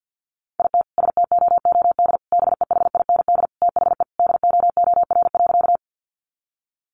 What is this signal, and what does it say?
Morse code, heard as a keyed tone, about 50 wpm, 700 Hz. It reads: IT STOOD BEHIND THE DOOR.
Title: Sentences from Top 500 Words 50wpm